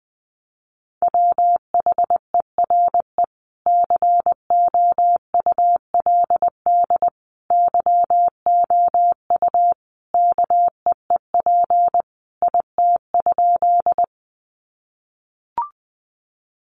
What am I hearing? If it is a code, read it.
Morse code, heard as a keyed tone, about 20 wpm, 700 Hz. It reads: WHERE COULD YOU KEEP IT?
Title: Sentences from Top 300 Words 20wpm